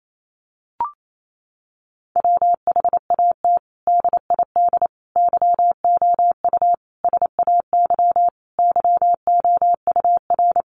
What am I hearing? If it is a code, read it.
Morse code, heard as a keyed tone, about 28 wpm, 700 Hz. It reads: WHAT DID YOU SAY YOUR
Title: Sentences from Top 200 Words 28wpm